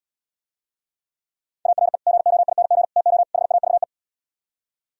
Morse code, heard as a keyed tone, about 77 wpm, 700 Hz. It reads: ARE GOING TO HAVE